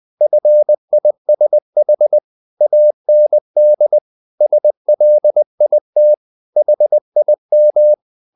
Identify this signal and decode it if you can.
Morse code, heard as a keyed tone, about 20 wpm, 600 Hz. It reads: FISH AND SLIT HIM
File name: the old man and the sea part 2 20wpm